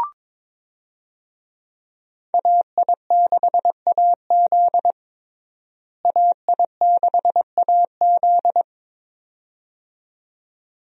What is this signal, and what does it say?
Morse code, heard as a keyed tone, about 22 wpm, 700 Hz. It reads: AI6AZ AI6AZ
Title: USA Call Signs - 5 Characters - Repeated 2x - 22wpm